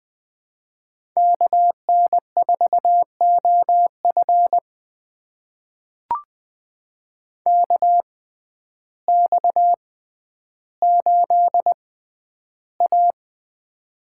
Morse code, heard as a keyed tone, about 20 wpm, spaced as for 11 wpm, 700 Hz. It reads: KN4OF KX8A
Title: USA Call Signs - 6x Character Spacing - 20wpm